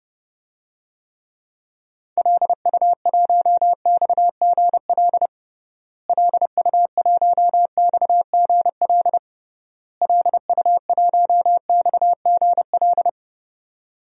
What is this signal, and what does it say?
Morse code, heard as a keyed tone, about 30 wpm, 700 Hz. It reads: LU1XGL LU1XGL LU1XGL
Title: DXCC-100 Call Signs - 6 Characters - Repeated 3x - 30wpm